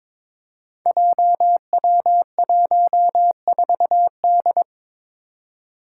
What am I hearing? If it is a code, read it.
Morse code, heard as a keyed tone, about 22 wpm, 700 Hz. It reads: JW14D
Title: Five Challenging Letter-Number Combinations 22wpm